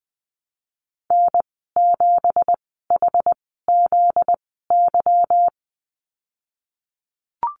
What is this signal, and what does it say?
Morse code, heard as a keyed tone, about 20 wpm, spaced as for 15 wpm, 700 Hz. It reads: N7HZY
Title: 245 Lesson 39g - Letter z - Call Signs 20wpm